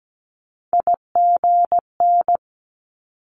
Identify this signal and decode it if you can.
Morse code, heard as a keyed tone, about 17 wpm, 700 Hz. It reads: IGN
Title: Multi-letter Phonemes 17wpm